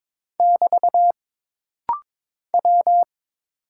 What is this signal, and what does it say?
Morse code, heard as a keyed tone, about 22 wpm, 700 Hz. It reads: <BT> W
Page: This rapid-fire timing is geared to push your ICR (Instant Character Recognition) skills.